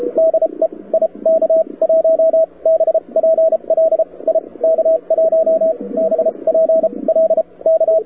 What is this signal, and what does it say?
Morse code, heard as a keyed tone, about 33 wpm, spaced as for 25 wpm, 650 Hz, with radio noise. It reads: DE IK1BPL IK1BPL K
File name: wwcw08 / k5zd wwcw08 1400z